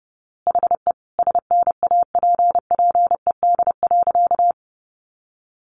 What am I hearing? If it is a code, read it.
Morse code, heard as a keyed tone, about 30 wpm, 700 Hz. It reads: HE SNAPPED.
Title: A Sound Of Thunder 30wpm